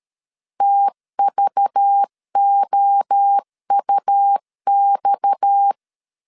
Morse code, heard as a keyed tone, about 13 wpm, 800 Hz. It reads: TVOUX
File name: CW / DET / 1 6 1500C S C 12wpm 3533